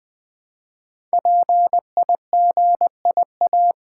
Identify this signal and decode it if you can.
Morse code, heard as a keyed tone, about 20 wpm, 700 Hz. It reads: PIGIA
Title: Code Groups - 5 Letters - 20wpm